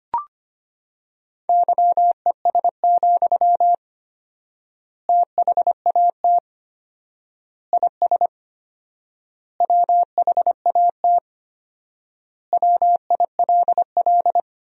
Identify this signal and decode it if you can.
Morse code, heard as a keyed tone, about 25 wpm, 700 Hz. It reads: YES, THAT IS WHAT WILL
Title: 4x Word-Spacing - Sentences from Top 500 Words - 25wpm